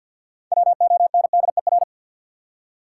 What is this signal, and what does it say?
Morse code, heard as a keyed tone, about 52 wpm, 700 Hz. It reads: WONDER
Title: Six Letter Words - 50wpm